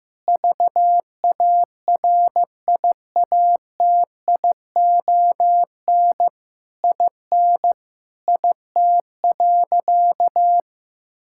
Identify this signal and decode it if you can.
Morse code, heard as a keyed tone, about 15 wpm, 700 Hz. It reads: VARIATION IN IT.
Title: Sentences from Top 1600 Words 15wpm